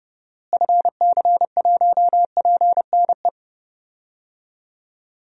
Morse code, heard as a keyed tone, about 30 wpm, 700 Hz. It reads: FC1PNE
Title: DXCC-100 Call Signs - 6 Characters - Repeated 2x - 30wpm